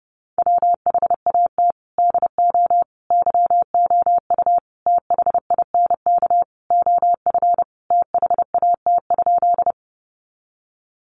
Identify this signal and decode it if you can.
Morse code, heard as a keyed tone, about 30 wpm, 700 Hz. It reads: WHAT DO YOU THINK OF THAT?
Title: Sentences from Top 200 Words 30wpm